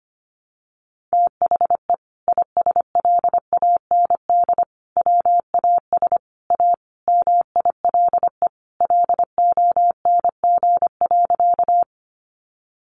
Morse code, heard as a keyed tone, about 25 wpm, 700 Hz. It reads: THE ISLAND WAS A MILE LONG.
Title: Sentences from Top 500 Words 25wpm